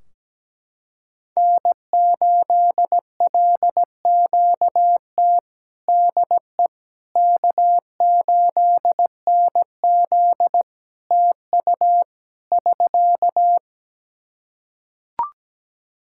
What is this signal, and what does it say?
Morse code, heard as a keyed tone, about 17 wpm, 700 Hz. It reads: N8LQT DE K8NZ TU <SK>
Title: Practice QSO - Exchange 4 - Ending - 17wpm